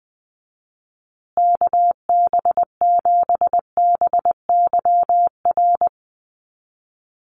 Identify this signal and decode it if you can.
Morse code, heard as a keyed tone, about 20 wpm, 700 Hz. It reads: KB7BYR
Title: USA Call Signs Repeated 2x 20wpm